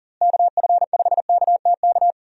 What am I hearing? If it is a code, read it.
Morse code, heard as a keyed tone, about 40 wpm, 700 Hz. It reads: KF5KTK
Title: USA Call Signs - 6 Characters - Repeated 3x - 40wpm